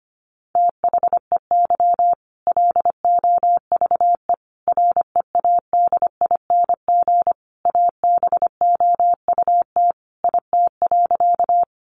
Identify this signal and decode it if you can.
Morse code, heard as a keyed tone, about 25 wpm, 700 Hz. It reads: THEY LOVE READING ABOUT IT.
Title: Sentences from Top 600 Words 25wpm